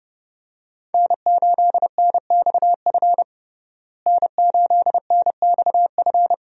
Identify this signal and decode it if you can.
Morse code, heard as a keyed tone, about 30 wpm, 700 Hz. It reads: N8NXF N8NXF
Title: USA Call Signs - 5 Characters - Repeated 2x - 30wpm